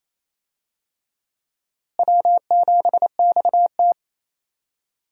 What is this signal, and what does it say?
Morse code, heard as a keyed tone, about 27 wpm, 700 Hz. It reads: W7XT
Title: USA Call Signs - 4 Characters - Repeated 2x - 28wpm